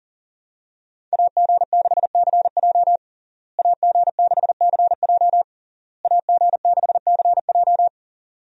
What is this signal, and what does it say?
Morse code, heard as a keyed tone, about 40 wpm, 700 Hz. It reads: AG6CJ AG6CJ AG6CJ
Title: USA Call Signs - 5 Characters - Repeated 3x - 40wpm